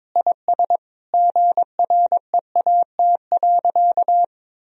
Morse code, heard as a keyed tone, about 22 wpm, 700 Hz. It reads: IS GREAT.
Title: Sentences from Top 1000 Words - Review Part 1 - 22wpm